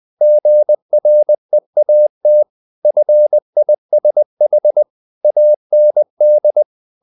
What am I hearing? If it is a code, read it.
Morse code, heard as a keyed tone, about 20 wpm, 600 Hz. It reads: GREAT FISH AND
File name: the old man and the sea / the old man and the sea part 1 20wpm